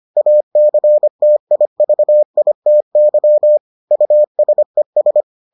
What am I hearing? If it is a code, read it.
Morse code, heard as a keyed tone, about 25 wpm, 600 Hz. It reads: ACTIVITY USES